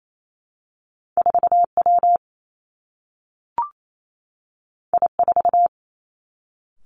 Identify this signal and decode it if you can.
Morse code, heard as a keyed tone, about 28 wpm, 700 Hz. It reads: 4 W I 4
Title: Two Challenging Letter-Number Combinations 28wpm